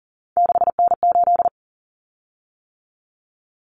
Morse code, heard as a keyed tone, about 40 wpm, 700 Hz. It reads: 6N8
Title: Common Cut Numbers - 3-Digit 40wpm